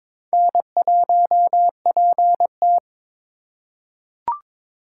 Morse code, heard as a keyed tone, about 22 wpm, 700 Hz. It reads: N1PT
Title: USA Call Signs - 4 Characters - 2x Character Spacing - 22wpm